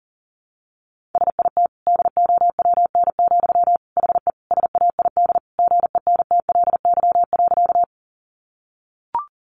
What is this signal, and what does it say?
Morse code, heard as a keyed tone, about 40 wpm, 700 Hz. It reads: SIT DOWN, HE SAID GENTLY.
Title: Sentences from Top 1500 Words 40wpm